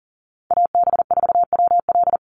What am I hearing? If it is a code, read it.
Morse code, heard as a keyed tone, about 40 wpm, 700 Hz. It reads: AB4WL
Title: NA SOTA Chaser Callsigns 40wpm